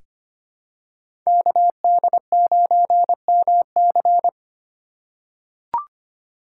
Morse code, heard as a keyed tone, about 25 wpm, 700 Hz. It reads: KD9MC
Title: USA Call Signs - 5 Characters - 3x Character Spacing - 25wpm